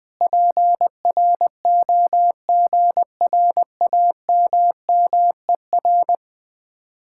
Morse code, heard as a keyed tone, about 20 wpm, 700 Hz. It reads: PROGRAMMER
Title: Speed-Racing Common QSO Phrases - 20wpm